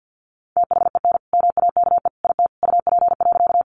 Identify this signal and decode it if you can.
Morse code, heard as a keyed tone, about 50 wpm, 700 Hz. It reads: THEN MAKE IT UP.
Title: Sentences from Top 100 Words 50wpm